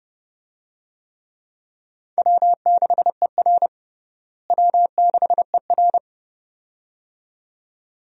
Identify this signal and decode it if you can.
Morse code, heard as a keyed tone, about 32 wpm, 700 Hz. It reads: W6ER W6ER
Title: USA Call Signs - 4 Characters - Repeated 2x - 30wpm